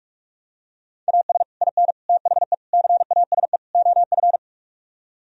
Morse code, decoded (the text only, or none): AS IN THE CASE OF